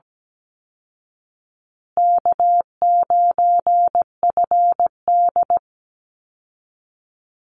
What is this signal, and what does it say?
Morse code, heard as a keyed tone, about 17 wpm, 700 Hz. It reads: K9FD
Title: Easy Call Signs Repeated 3x - 17wpm